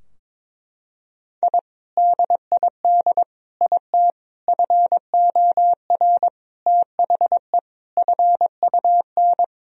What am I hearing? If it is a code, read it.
Morse code, heard as a keyed tone, about 22 wpm, 700 Hz. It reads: I DID IT FOR THE FUN
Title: Sentences from Top 1000 Words - Review Part 4 - 22wpm